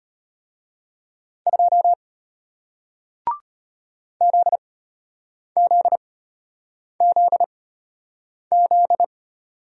Morse code, heard as a keyed tone, about 33 wpm, spaced as for 8 wpm, 700 Hz. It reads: 2 ZZZZ